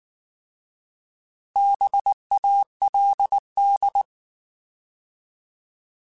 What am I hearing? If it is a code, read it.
Morse code, heard as a keyed tone, about 19 wpm, 800 Hz. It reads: BALD
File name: Alphabetical / Alphabetical mp3s / Bs @18